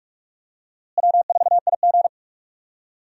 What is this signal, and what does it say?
Morse code, heard as a keyed tone, about 46 wpm, 700 Hz. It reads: W4IG
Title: USA Call Signs - 4 Characters - Repeated 3x - 45wpm